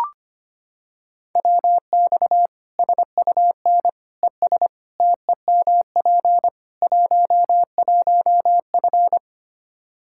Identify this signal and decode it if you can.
Morse code, heard as a keyed tone, about 25 wpm, 700 Hz. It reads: WX SUN ES TEMP 11F
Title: Practice QSO - Exchange 2 - Info - 25wpm